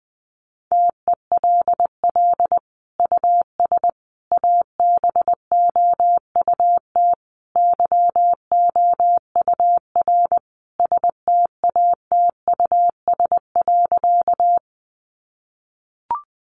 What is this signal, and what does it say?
Morse code, heard as a keyed tone, about 20 wpm, 700 Hz. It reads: TELL US ABOUT YOUR STATUS.
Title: Sentences from Top 1400 Words 20wpm